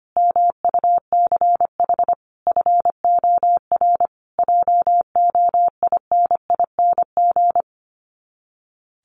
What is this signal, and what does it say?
Morse code, heard as a keyed tone, about 25 wpm, 700 Hz. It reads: MUCH FOR JOINING
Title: Sets of 5 Words Encore2 25wpm